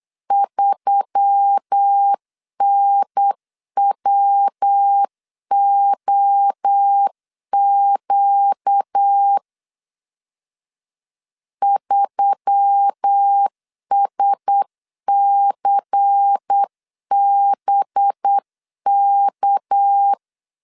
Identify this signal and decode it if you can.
Morse code, heard as a keyed tone, about 8 wpm, 800 Hz. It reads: 3NWOQ 3SCBK
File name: CW / DET / 1 6 1500C S C 8wpm 5219